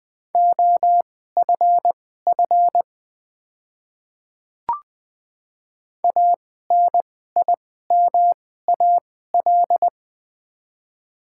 Morse code, heard as a keyed tone, about 20 wpm, spaced as for 15 wpm, 700 Hz. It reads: OFF ANIMAL
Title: 2x Character Spacing - Top 200 Words - 20wpm